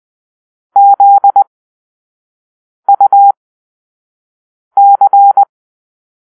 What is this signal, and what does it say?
Morse code, heard as a keyed tone, about 20 wpm, spaced as for 5 wpm, 800 Hz. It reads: ZUC
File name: arrl / Grp-5 Drill-3